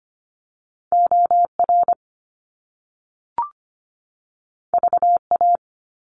Morse code, heard as a keyed tone, about 25 wpm, 700 Hz. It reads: OR VA
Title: US States 25wpm